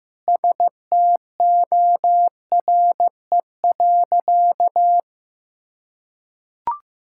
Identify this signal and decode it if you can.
Morse code, heard as a keyed tone, about 15 wpm, 700 Hz. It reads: STORE.
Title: Sentences from Top 1000 Words - Review Part 5 - 15wpm